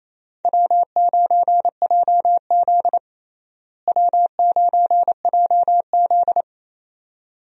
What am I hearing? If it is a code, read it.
Morse code, heard as a keyed tone, about 28 wpm, 700 Hz. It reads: W9JZ W9JZ
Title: USA Call Signs - 4 Characters - Repeated 2x - 28wpm